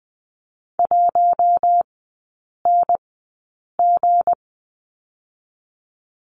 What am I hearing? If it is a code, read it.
Morse code, heard as a keyed tone, about 20 wpm, spaced as for 8 wpm, 690 Hz. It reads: 1NG
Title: 159 Lesson 26e - Letter g - Three Characters 20wpm